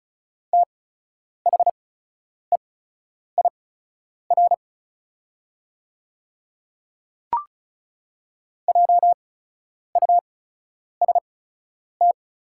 Morse code, heard as a keyed tone, about 36 wpm, spaced as for 10 wpm, 690 Hz. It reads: THEIR JUST
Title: 8x Character Spacing - Top 200 Words - 35wpm